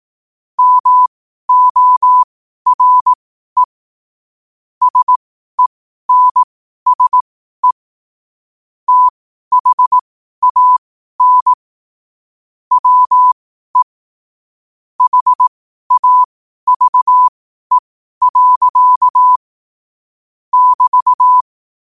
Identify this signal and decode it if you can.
Morse code, heard as a keyed tone, about 18 wpm, spaced as for 12 wpm, 1000 Hz. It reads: MORE SENSE THAN WE HAVE. <BT>
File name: 12wpm